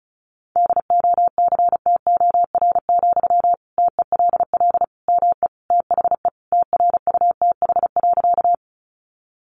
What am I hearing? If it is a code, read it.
Morse code, heard as a keyed tone, about 34 wpm, 700 Hz. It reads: DOCTOR, TELL ME THE TRUTH.
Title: Sentences from Top 1500 Words 35wpm